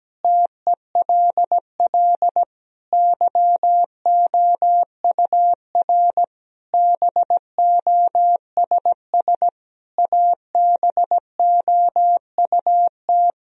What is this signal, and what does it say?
Morse code, heard as a keyed tone, about 17 wpm, 700 Hz. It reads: TELL YOUR BOSS ABOUT
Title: Sentences from Top 1000 Words - Review Part 4 - 17wpm